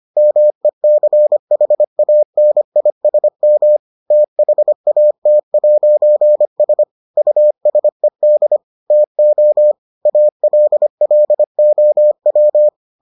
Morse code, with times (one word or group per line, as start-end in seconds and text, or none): 0.2-3.8 MECHANISM
4.1-6.8 THAT'S
7.2-8.6 USED
8.9-9.7 TO
10.1-12.7 ALLOW